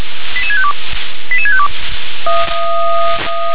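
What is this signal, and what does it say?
Five-tone selective call (ZVEI), heard as quick runs of five tones, tones of about 80 ms. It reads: 88652 88652
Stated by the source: ZVEI five-tone selective call 88693 received on an SDR.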